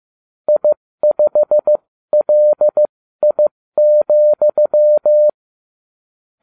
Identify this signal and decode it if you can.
Morse code, heard as a keyed tone, about 15 wpm, spaced as for 13 wpm, 600 Hz. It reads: I5LI,